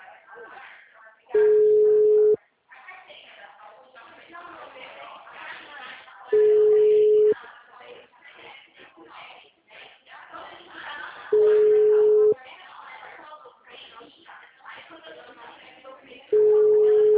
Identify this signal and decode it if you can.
Ringback tone (Europe, 425 Hz), with speech.
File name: out- -2011- - - .47333